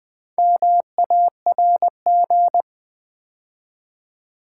Morse code, heard as a keyed tone, about 20 wpm, 700 Hz. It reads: MARG
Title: Build A Word - 2 Letters At A Time - 8-Letter Words - 20wpm